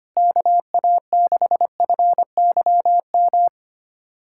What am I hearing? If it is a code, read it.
Morse code, heard as a keyed tone, about 25 wpm, 700 Hz. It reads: KA6FYM
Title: USA Callsign DE Callsign - 6 Characters - 25wpm